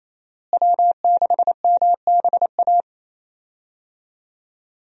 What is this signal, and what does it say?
Morse code, heard as a keyed tone, about 28 wpm, 700 Hz. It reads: W6MBA
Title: USA Call Signs - 5 Characters - Repeated 2x - 28wpm